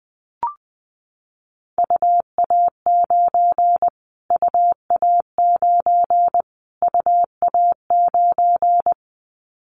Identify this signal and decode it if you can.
Morse code, heard as a keyed tone, about 20 wpm, 700 Hz. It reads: UA9 UA9 UA9
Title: Callsign Prefixes for Top 25 DXCC - Repeated 3x - 20wpm